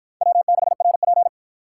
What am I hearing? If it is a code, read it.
Morse code, heard as a keyed tone, about 53 wpm, 700 Hz. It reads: W6RP